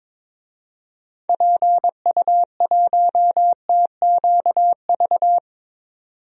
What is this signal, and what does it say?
Morse code, heard as a keyed tone, about 22 wpm, 700 Hz. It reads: PU1TQV
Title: DXCC-100 Call Signs - 6 Characters - 22wpm